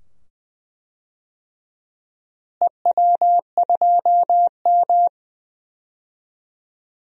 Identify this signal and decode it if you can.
Morse code, heard as a keyed tone, about 21 wpm, 700 Hz. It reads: EW2M
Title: DXCC-100 Call Signs - 4 Characters - Repeated 2x - 20wpm